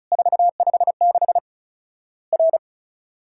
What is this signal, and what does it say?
Morse code, heard as a keyed tone, about 36 wpm, 700 Hz. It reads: 456 R
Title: Worked All Europe DX Contest - QTC Exchange - 35wpm